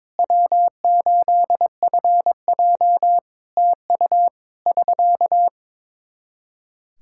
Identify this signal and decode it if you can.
Morse code, heard as a keyed tone, about 22 wpm, 700 Hz. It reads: W8FJ TU <SK>
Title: Practice QSO - Exchange 4 - Ending - 22wpm